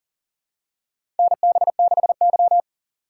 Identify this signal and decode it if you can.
Morse code, heard as a keyed tone, about 40 wpm, 700 Hz. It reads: NB6Y
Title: USA Call Signs - 4 Characters - 3x Character Spacing - 40wpm